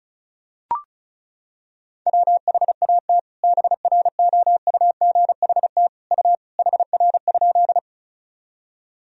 Morse code, heard as a keyed tone, about 36 wpm, 700 Hz. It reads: WHAT BROUGHT U HR?
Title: Sentences from Top 500 Words and Abbreviations 35wpm